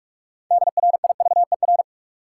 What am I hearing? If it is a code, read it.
Morse code, heard as a keyed tone, about 45 wpm, 700 Hz. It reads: DRIVER
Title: Sentences from Top 1000 Words - New Word Drill - 45wpm